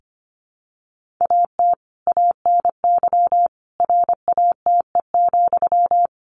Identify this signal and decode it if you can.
Morse code, heard as a keyed tone, about 25 wpm, 700 Hz. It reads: AT ANY RATE,
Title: Sentences from Top 1500 Words 25wpm